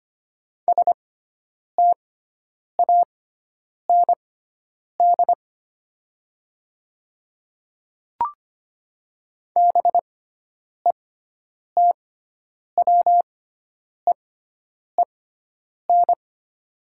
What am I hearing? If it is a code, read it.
Morse code, heard as a keyed tone, about 25 wpm, spaced as for 9 wpm, 700 Hz. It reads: STAND BETWEEN
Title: 6x Character Spacing - Top 300 Words - 25wpm